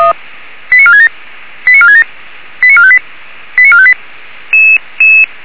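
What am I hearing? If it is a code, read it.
Five-tone selective call (ZVEI), heard as quick runs of five tones, tones of about 70 ms. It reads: 89467 89467 89458 89458
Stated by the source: ZVEI five-tone selective call received on an SDR.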